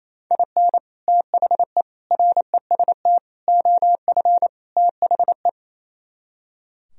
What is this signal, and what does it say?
Morse code, heard as a keyed tone, about 35 wpm, 700 Hz. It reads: IN THE REST OF THE
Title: Speed-Racing - Sets of 5 Words 28wpm